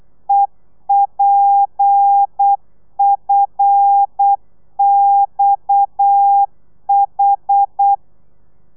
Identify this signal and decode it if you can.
Morse code, heard as a keyed tone, about 8 wpm, 800 Hz. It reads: EPFXH